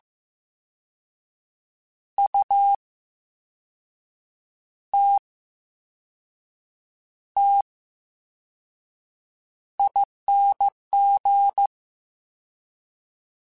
Morse code, heard as a keyed tone, about 15 wpm, spaced as for 5 wpm, 800 Hz. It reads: UTTING
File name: Special INGs-mp3s / Special INGs-mp3s / VerySlowToFaster Ing @13